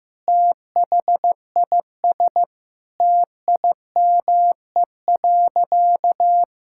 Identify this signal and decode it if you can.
Morse code, heard as a keyed tone, about 15 wpm, 700 Hz. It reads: THIS TIME.
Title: Sentences from Top 1000 Words - Review Part 5 - 15wpm